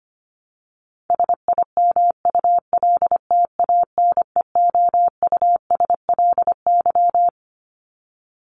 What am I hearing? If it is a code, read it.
Morse code, heard as a keyed tone, about 25 wpm, 700 Hz. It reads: SIMULTANEOUSLY
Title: LY Suffix - Top 100 Words with Base Words 25wpm